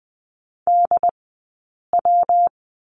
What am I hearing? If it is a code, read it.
Morse code, heard as a keyed tone, about 20 wpm, spaced as for 8 wpm, 700 Hz. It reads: DW
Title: 151 Lesson 25d - Letter p - Two Characters 20wpm